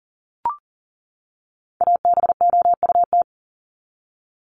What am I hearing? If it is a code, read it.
Morse code, heard as a keyed tone, about 40 wpm, 700 Hz. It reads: ABOUT
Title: Top 500 Words 40wpm